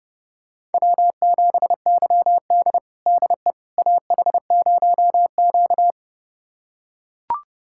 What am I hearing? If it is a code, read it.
Morse code, heard as a keyed tone, about 32 wpm, 700 Hz. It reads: W7YD DE AH0Q
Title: USA Callsign DE Callsign - 4 Characters - 30wpm